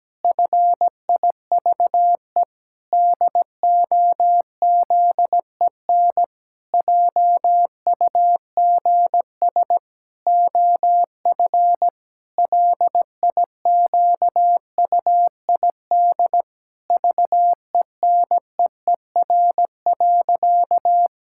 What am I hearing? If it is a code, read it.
Morse code, heard as a keyed tone, about 17 wpm, 700 Hz. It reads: FIVE DOZEN JUGS OF LIQUID VENEER.
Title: Pangram Challenge v2 - 17wpm